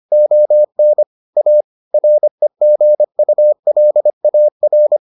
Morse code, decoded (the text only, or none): ON A REGULAR